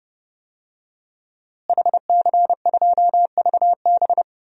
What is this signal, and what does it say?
Morse code, heard as a keyed tone, about 30 wpm, 700 Hz. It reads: HC2VB
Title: DXCC-100 Call Signs - 5 Characters - Repeated 2x - 30wpm